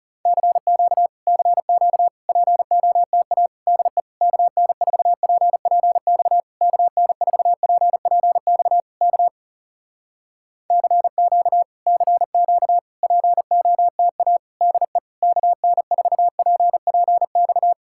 Morse code, heard as a keyed tone, about 40 wpm, 700 Hz. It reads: CQ CQ POTA DE KN4PPX KN4PPX K CQ CQ POTA DE KN4PPX